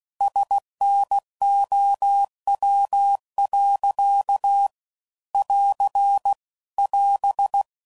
Morse code, heard as a keyed tone, about 16 wpm, 800 Hz. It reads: SNOW. <AR> <AS>
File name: CW Code Groups @15WPM / Weather Sets / Weather 3 17